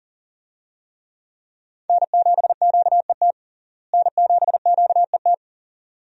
Morse code, heard as a keyed tone, about 40 wpm, 700 Hz. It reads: N7QET N7QET
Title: USA Call Signs - 5 Characters - Repeated 2x - 40wpm